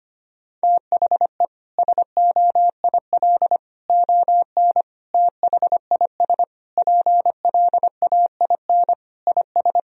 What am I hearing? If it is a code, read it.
Morse code, heard as a keyed tone, about 25 wpm, 700 Hz. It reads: THE SOIL ON THIS PLAIN IS
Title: Sentences from Top 1000 Words - Review Part 4 - 25wpm